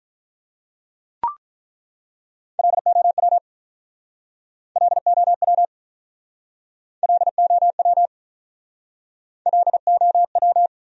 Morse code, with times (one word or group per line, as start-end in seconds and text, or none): 2.6-3.4 LOW
4.8-5.7 LOW
7.0-8.1 LOW
9.5-10.7 LOW